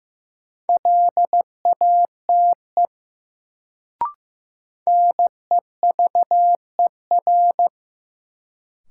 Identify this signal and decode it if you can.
Morse code, heard as a keyed tone, about 15 wpm, 700 Hz. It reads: LATE NEVER
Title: Sentences from Top 300 Words - New Word Drill - 15wpm